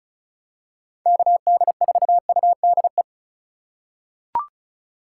Morse code, heard as a keyed tone, about 36 wpm, 700 Hz. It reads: KD4UDE
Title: USA Call Signs - 6 Characters - 3x Character Spacing - 35wpm